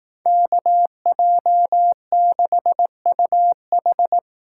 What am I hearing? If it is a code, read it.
Morse code, heard as a keyed tone, about 19 wpm, 700 Hz. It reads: KJ6UH
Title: Speed Racing - USA 2x2 Call Signs 15wpm